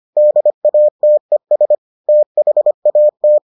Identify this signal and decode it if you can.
Morse code, heard as a keyed tone, about 25 wpm, 600 Hz. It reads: DATES THAT